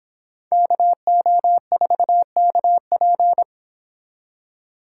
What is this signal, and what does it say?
Morse code, heard as a keyed tone, about 26 wpm, 700 Hz. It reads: KO4KP